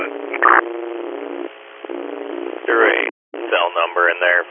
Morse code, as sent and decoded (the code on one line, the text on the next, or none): none